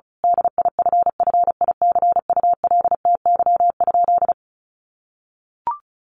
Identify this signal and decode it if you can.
Morse code, heard as a keyed tone, about 34 wpm, 700 Hz. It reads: DIFFICULTY?
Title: Sentences from Top 1500 Words 35wpm